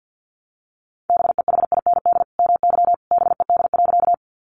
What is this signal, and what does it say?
Morse code, heard as a keyed tone, about 50 wpm, 700 Hz. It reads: BEHIND MY BED.
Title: Sentences from Top 500 Words 50wpm